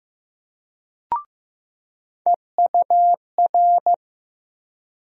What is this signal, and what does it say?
Morse code, heard as a keyed tone, about 15 wpm, 700 Hz. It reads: EUR
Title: USA Call Sign Postfixes - 3 Letters 15wpm